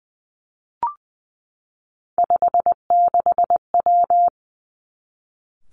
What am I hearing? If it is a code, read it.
Morse code, heard as a keyed tone, about 20 wpm, 700 Hz. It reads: 56W
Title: Three Challenging Letter-Number Combinations 20wpm